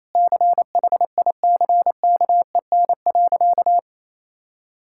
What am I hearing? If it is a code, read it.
Morse code, heard as a keyed tone, about 28 wpm, 700 Hz. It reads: CHICKEN.
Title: Sentences from Top 1000 Words - Review Part 5 - 28wpm